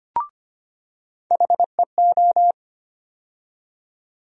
Morse code, heard as a keyed tone, about 25 wpm, 700 Hz. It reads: HEO
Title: USA Call Sign Postfixes - 3 Letters 25wpm